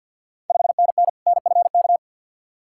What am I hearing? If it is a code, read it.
Morse code, heard as a keyed tone, about 51 wpm, 690 Hz. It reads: 5NN NFK
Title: Ontario QSO Party Exchange 50wpm